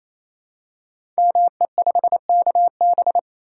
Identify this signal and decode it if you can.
Morse code, heard as a keyed tone, about 27 wpm, 700 Hz. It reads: ME5KB
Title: Code Groups - 5 Letter:Numbers - 28wpm